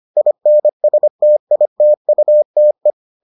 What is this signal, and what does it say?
Morse code, heard as a keyed tone, about 25 wpm, 600 Hz. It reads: INSTITUTE